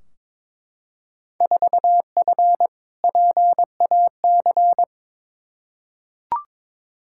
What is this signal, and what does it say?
Morse code, heard as a keyed tone, about 22 wpm, 700 Hz. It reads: 4F PAC
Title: Field Day Exchanges Encore - 22wpm